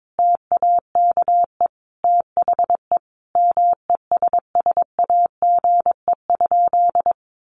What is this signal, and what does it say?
Morse code, heard as a keyed tone, about 22 wpm, 700 Hz. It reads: TAKE THE MESSAGE?
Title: Sentences from Top 1100 Words 22wpm